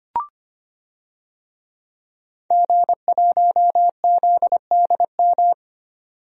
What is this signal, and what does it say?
Morse code, heard as a keyed tone, about 25 wpm, 700 Hz. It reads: G1ZDM
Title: DXCC-100 Call Signs - 5 Characters - 25wpm